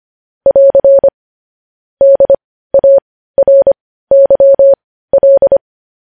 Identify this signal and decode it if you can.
Morse code, heard as a keyed tone, about 25 wpm, spaced as for 15 wpm, 550 Hz. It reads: <AR> DARYL
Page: news headlines in morse code at 15 wpm